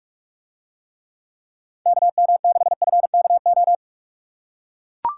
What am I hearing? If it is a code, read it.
Morse code, heard as a keyed tone, about 40 wpm, 700 Hz. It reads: KM6FKO